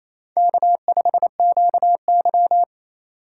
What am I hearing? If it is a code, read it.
Morse code, heard as a keyed tone, about 28 wpm, 700 Hz. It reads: K5QY
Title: USA Call Signs - 4x Character Spacing - 28wpm